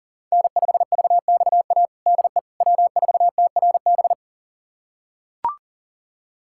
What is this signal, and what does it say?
Morse code, heard as a keyed tone, about 40 wpm, 700 Hz. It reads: N5VXA DE W4TRB
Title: USA Callsign DE Callsign - 5 Characters - 40wpm